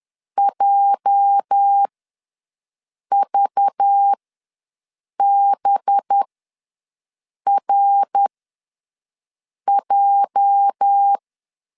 Morse code, heard as a keyed tone, about 10 wpm, spaced as for 5 wpm, 800 Hz. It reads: JVBRJ
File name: LEZIONI / 8 YJ45 / 8 YJ ws 10wpm 4043